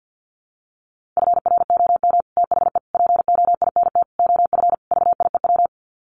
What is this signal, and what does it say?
Morse code, heard as a keyed tone, about 50 wpm, 700 Hz. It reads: FROM THE POINT OF VIEW
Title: Sets of 5 Words 50wpm